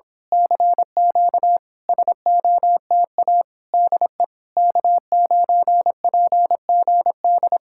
Morse code, heard as a keyed tone, about 26 wpm, 700 Hz. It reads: CQ SOTA DE K9PGD
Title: Speed-Racing Common QSO Phrases Encore - 22wpm